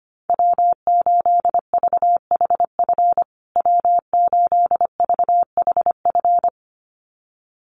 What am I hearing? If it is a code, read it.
Morse code, heard as a keyed tone, about 25 wpm, 700 Hz. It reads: W8VHF W8VHF
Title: USA Call Signs Repeated 2x 25wpm